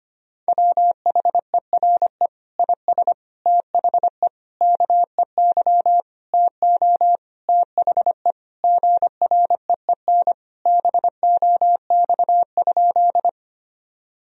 Morse code, heard as a keyed tone, about 25 wpm, 700 Hz. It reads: WHERE IS THE KEY TO THE GREEN BOX?
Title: Sentences from Top 1000 Words - Review Part 4 - 25wpm